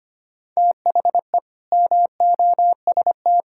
Morse code, heard as a keyed tone, about 25 wpm, 700 Hz. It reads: THE MOST
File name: practice / 25wpm / Sentences with 4-Word Phrases - 25wpm